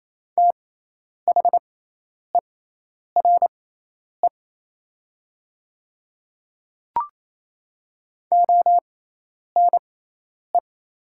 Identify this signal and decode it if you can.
Morse code, heard as a keyed tone, about 27 wpm, spaced as for 10 wpm, 690 Hz. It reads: THERE ONE
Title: 6x Character Spacing - Top 100 Words - 28wpm